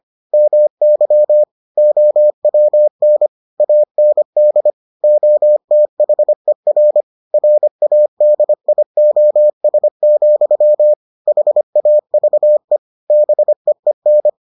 Morse code, heard as a keyed tone, about 25 wpm, 600 Hz. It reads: MY OWN AND OTHER RADIOS, HAVE BEEN